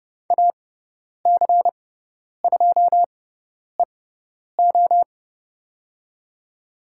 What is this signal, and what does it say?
Morse code, heard as a keyed tone, about 30 wpm, spaced as for 10 wpm, 700 Hz. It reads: AC2EO
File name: Lesson 18g - Letter m - Call Signs 30wpm